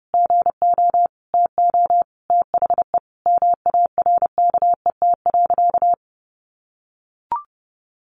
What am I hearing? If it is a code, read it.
Morse code, heard as a keyed tone, about 30 wpm, 700 Hz. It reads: GO TO THE MARKET.
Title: Sentences from Top 1400 Words 30wpm